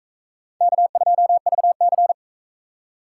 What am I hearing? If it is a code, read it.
Morse code, heard as a keyed tone, about 37 wpm, 700 Hz. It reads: K2VC